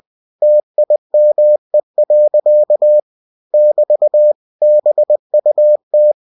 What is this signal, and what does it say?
Morse code, heard as a keyed tone, about 20 wpm, 600 Hz. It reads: TIME. <BT> BUT
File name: cw-books / winnie-the-pooh / winnie-20wpm-page-0014